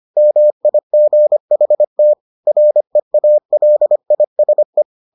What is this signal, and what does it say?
Morse code, heard as a keyed tone, about 25 wpm, 600 Hz. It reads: MIGHT REALISE